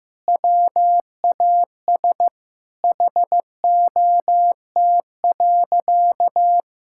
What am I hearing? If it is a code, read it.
Morse code, heard as a keyed tone, about 16 wpm, 700 Hz. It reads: WAS HOT.
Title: Sentences from Top 1000 Words - Review Part 5 - 15wpm